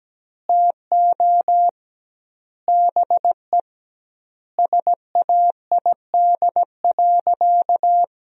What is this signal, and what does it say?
Morse code, heard as a keyed tone, about 17 wpm, 700 Hz. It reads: TO BE SAID.
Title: 2x Word-Spacing - Sentences from Top 100 Words - 17wpm